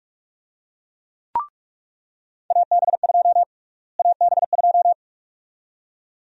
Morse code, heard as a keyed tone, about 46 wpm, 700 Hz. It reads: AB2 AB2
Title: USA Call Sign Prefixes - Repeated 2x - 45wpm